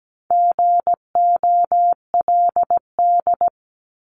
Morse code, heard as a keyed tone, about 17 wpm, 700 Hz. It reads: GOLD
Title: Multi-letter Phonemes Repeated 3x 17wpm